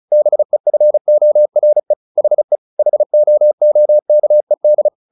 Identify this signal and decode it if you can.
Morse code, heard as a keyed tone, about 34 wpm, 600 Hz. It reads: BEFORE HE HOOKED
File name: the old man and the sea / the old man and the sea 1 35wpm